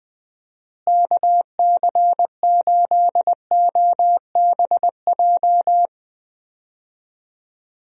Morse code, heard as a keyed tone, about 20 wpm, 700 Hz. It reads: KC8OBJ
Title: USA Call Signs - 6 Characters - Repeated 2x - 20wpm